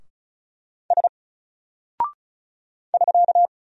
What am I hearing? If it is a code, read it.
Morse code, heard as a keyed tone, about 36 wpm, 700 Hz. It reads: S <SK>
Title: Single Character (Letter:Number:Punctuation:Prosign) Alt 35wpm